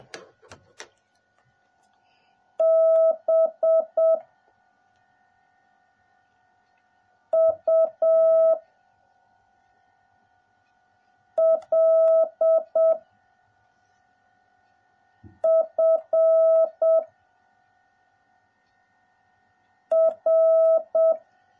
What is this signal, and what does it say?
Morse code, heard as a keyed tone, about 7 wpm, 650 Hz. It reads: B U L F R